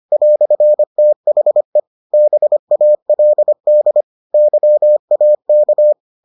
Morse code, heard as a keyed tone, about 25 wpm, 600 Hz. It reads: "THE BALD YAK